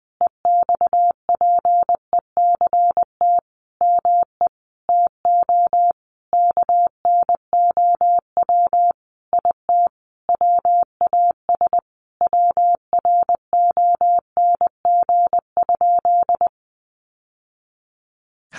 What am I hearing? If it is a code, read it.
Morse code, heard as a keyed tone, about 20 wpm, 700 Hz. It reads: EXPECT ME TO KNOW IT WAS WRONG?
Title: Sentences from Top 1100 Words 20wpm